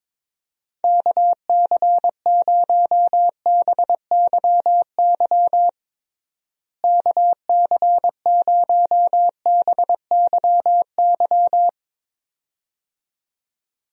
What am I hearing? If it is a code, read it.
Morse code, heard as a keyed tone, about 21 wpm, 700 Hz. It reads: KC0BYY KC0BYY
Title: USA Call Signs - 6 Characters - Repeated 2x - 22wpm